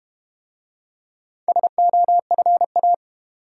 Morse code, heard as a keyed tone, about 34 wpm, 700 Hz. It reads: SOFA